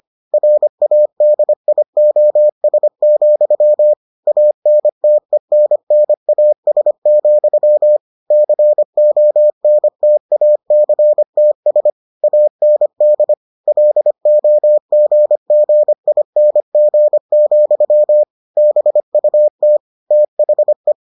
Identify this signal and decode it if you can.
Morse code, heard as a keyed tone, about 25 wpm, 600 Hz. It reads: RADIOS, ANTENNAS, CONTACTS AND LOGGING, BUT THE